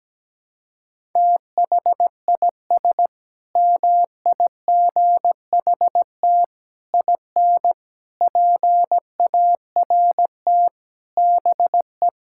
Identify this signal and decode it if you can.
Morse code, heard as a keyed tone, about 17 wpm, 700 Hz. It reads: THIS MIGHT IN PART BE
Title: Sentences from Top 1000 Words - Review Part 5 - 17wpm